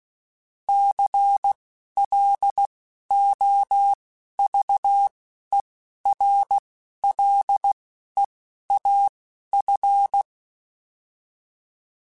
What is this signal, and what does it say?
Morse code, heard as a keyed tone, about 16 wpm, spaced as for 11 wpm, 800 Hz. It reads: CLOVERLEAF
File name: fromSMs / FromSMs mp3s / Cities@10